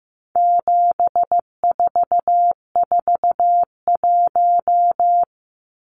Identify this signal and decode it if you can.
Morse code, heard as a keyed tone, about 15 wpm, 700 Hz. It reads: 7441
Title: Numbers - 4-Digit 15wpm